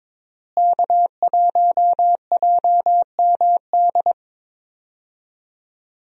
Morse code, heard as a keyed tone, about 21 wpm, 700 Hz. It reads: K1JMD
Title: USA Call Signs - 5 Characters - Repeated 2x - 22wpm